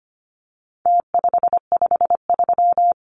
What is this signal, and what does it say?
Morse code, heard as a keyed tone, about 25 wpm, 700 Hz. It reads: T553
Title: Numbers - Contest Style - 4-Digit 25wpm